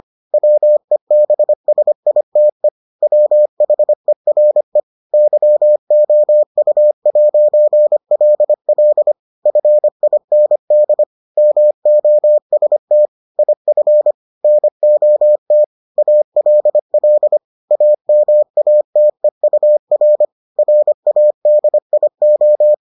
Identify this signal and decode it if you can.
Morse code, heard as a keyed tone, about 25 wpm, 600 Hz. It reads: WEBSITE WHERE YOU'LL FIND MOST IF NOT ALL AMATEUR RADIO